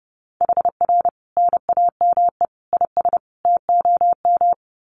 Morse code, heard as a keyed tone, about 30 wpm, 700 Hz. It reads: HR NAME IS TOM
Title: Common QSO Phrases 30wpm